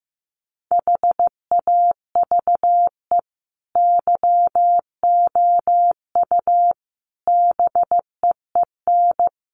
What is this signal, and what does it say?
Morse code, heard as a keyed tone, about 15 wpm, 700 Hz. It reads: HAVE YOU BEEN
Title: Sentences from Top 700 Words 15wpm